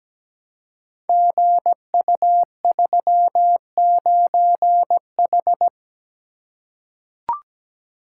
Morse code, heard as a keyed tone, about 17 wpm, 700 Hz. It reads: GU39H
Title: Code Groups - 5 Letter:Numbers - 17wpm